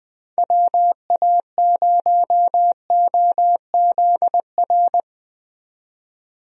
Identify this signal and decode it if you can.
Morse code, heard as a keyed tone, about 20 wpm, 700 Hz. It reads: WA0OZR
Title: USA Call Signs - 6 Characters - 20wpm